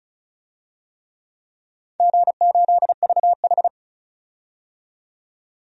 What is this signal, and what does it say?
Morse code, heard as a keyed tone, about 36 wpm, 700 Hz. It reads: G8VH
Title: DXCC-100 Call Signs - 4 Characters - 35wpm